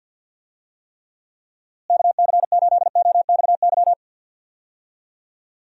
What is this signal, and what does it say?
Morse code, heard as a keyed tone, about 51 wpm, 700 Hz. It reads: KC8OXY
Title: USA Call Signs - 6 Characters - Repeated 3x - 50wpm